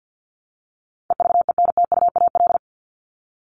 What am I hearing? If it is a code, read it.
Morse code, heard as a keyed tone, about 48 wpm, 700 Hz. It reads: EVENTUAL
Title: LY Suffix - Top 100 Words with Base Words 50wpm